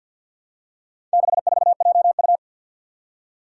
Morse code, heard as a keyed tone, about 53 wpm, 700 Hz. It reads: 64JU
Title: Four Challenging Letter-Number Combinations 50wpm